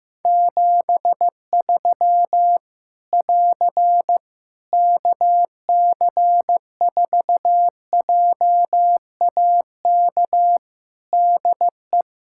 Morse code, heard as a keyed tone, about 15 wpm, 700 Hz. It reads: 73 <AR> KC4JAK DE
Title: Practice QSO - Exchange 4 - Ending - 15wpm